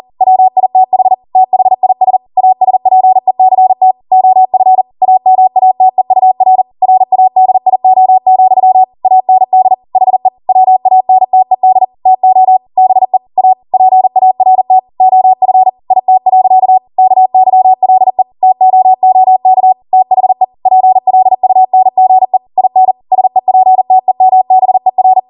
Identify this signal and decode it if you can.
Morse code, heard as a keyed tone, about 40 wpm, 750 Hz. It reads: WITH THIS ASPECT OF AMATEUR RADIO, AND HE WANTED TO BE A PART OF IT. KYLE TOOK THE PLUNGE IN SEPTEMBER